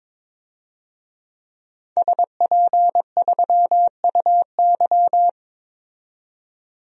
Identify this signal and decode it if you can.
Morse code, heard as a keyed tone, about 22 wpm, 700 Hz. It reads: SP3UY
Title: DXCC-100 Call Signs - 5 Characters - 22wpm